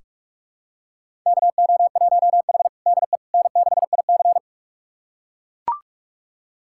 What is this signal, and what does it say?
Morse code, heard as a keyed tone, about 44 wpm, 690 Hz. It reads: KO1H DE N6IC
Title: USA Callsign DE Callsign - 4 Characters - 45wpm